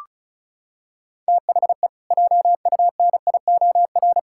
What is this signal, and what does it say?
Morse code, heard as a keyed tone, about 36 wpm, 700 Hz. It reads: THE JUNIOR
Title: Pangram Challenge v2 - 35wpm